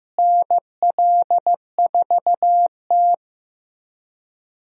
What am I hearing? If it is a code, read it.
Morse code, heard as a keyed tone, about 15 wpm, 700 Hz. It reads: NL4T
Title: USA Call Signs - 4 Characters - Repeated 3x - 15wpm